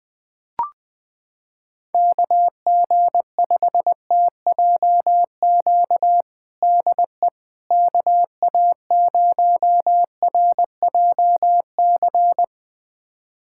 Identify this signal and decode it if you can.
Morse code, heard as a keyed tone, about 20 wpm, 700 Hz. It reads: KG5TJQ DE KA0RJC
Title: USA Callsign DE Callsign - 6 Characters - 20wpm